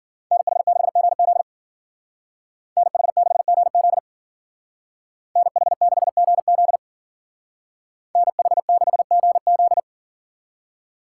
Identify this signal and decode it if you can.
Morse code, heard as a keyed tone, about 52 wpm, 700 Hz. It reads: NH6GZ NH6GZ NH6GZ NH6GZ